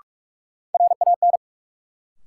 Morse code, heard as a keyed tone, about 46 wpm, 700 Hz. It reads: RAN
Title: Sentences from Top 500 Words - New Word Drill - 45wpm